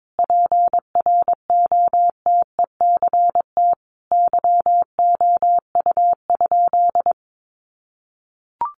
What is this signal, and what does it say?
Morse code, heard as a keyed tone, about 22 wpm, 700 Hz. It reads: PROTECT YOU?
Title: Sentences from Top 900 Words 22wpm